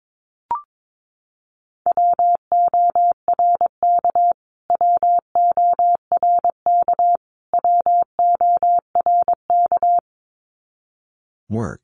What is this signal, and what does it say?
Morse code, heard as a keyed tone, about 22 wpm, 700 Hz. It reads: WORK WORK WORK
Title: Top 500 Words Repeated 22wpm